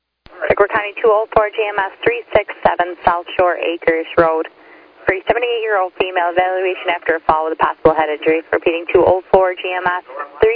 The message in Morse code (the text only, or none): none